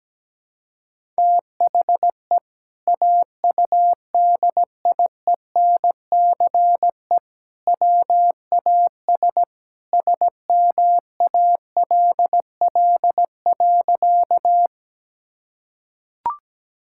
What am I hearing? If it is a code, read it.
Morse code, heard as a keyed tone, about 17 wpm, 700 Hz. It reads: THE AUDIENCE WAS SMALL.
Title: Sentences from Top 1000 Words - Review Part 5 - 17wpm